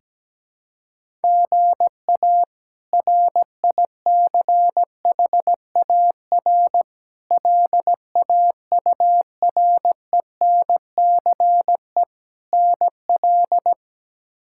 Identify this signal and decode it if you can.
Morse code, heard as a keyed tone, about 17 wpm, 700 Hz. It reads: GA RICHAR LAURENCE NL
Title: K1USN SST QSO Practice - 17wpm